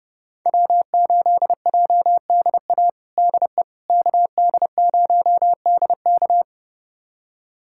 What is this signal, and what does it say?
Morse code, heard as a keyed tone, about 30 wpm, 700 Hz. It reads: W8JDA DE KD0DK
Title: USA Callsign DE Callsign - 5 Characters - 30wpm